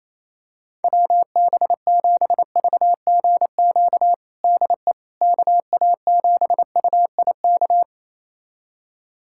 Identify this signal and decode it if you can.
Morse code, heard as a keyed tone, about 28 wpm, 700 Hz. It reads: WB7VGQ DE KA7UIK
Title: USA Callsign DE Callsign - 6 Characters - 28wpm